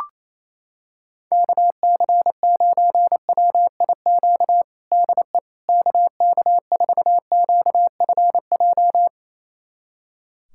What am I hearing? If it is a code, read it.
Morse code, heard as a keyed tone, about 27 wpm, 700 Hz. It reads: KC9WIQ DE KK4QFJ
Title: USA Callsign DE Callsign - 6 Characters - 28wpm